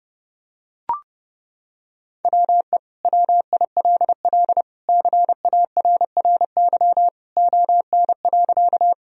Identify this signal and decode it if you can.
Morse code, heard as a keyed tone, about 32 wpm, 700 Hz. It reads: WE WILL CARRY ON.
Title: Sentences from Top 1000 Words - Review Part 1 - 30wpm